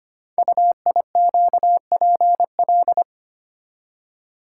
Morse code, heard as a keyed tone, about 25 wpm, 700 Hz. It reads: UIQPL
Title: Code Groups - 5 Letters - 25wpm